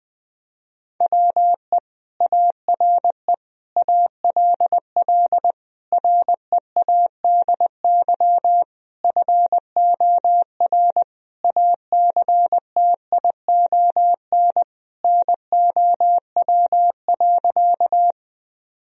Morse code, heard as a keyed tone, about 20 wpm, 700 Hz. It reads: WE ARE ALL READY FOR ACTION NOW.
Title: Sentences from Top 1000 Words - Review Part 4 - 20wpm